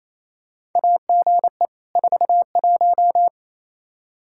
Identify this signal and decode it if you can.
Morse code, heard as a keyed tone, about 28 wpm, 700 Hz. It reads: AGE 41
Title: Practice QSO - Exchange 3 - More Info - 28wpm